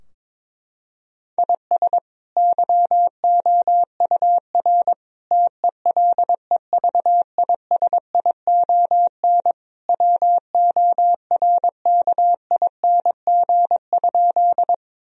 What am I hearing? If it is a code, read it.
Morse code, heard as a keyed tone, about 22 wpm, 700 Hz. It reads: IS YOUR TELEVISION WORKING?
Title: Sentences from Top 1000 Words - Review Part 4 - 22wpm